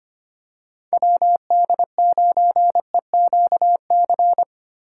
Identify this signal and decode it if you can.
Morse code, heard as a keyed tone, about 25 wpm, 700 Hz. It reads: WD9EQC
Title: USA Call Signs - 6 Characters - 3x Character Spacing - 25wpm